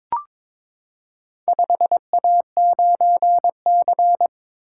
Morse code, heard as a keyed tone, about 22 wpm, 700 Hz. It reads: 5A9C
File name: practice / 22wpm / Four Letter-Number Combinations - 22wpm